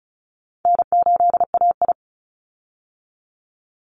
Morse code, heard as a keyed tone, about 34 wpm, 700 Hz. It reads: N8AI
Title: Easy Call Signs Repeated 2x - 35wpm